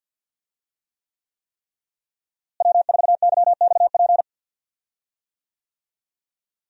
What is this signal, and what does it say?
Morse code, heard as a keyed tone, about 49 wpm, 700 Hz. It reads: W4YXP
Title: USA Call Signs - 5 Characters - Repeated 3x - 50wpm